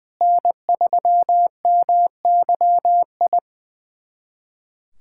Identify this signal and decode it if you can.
Morse code, heard as a keyed tone, about 20 wpm, 700 Hz. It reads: N3MYI
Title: USA Callsign DE Callsign - 5 Characters - 20wpm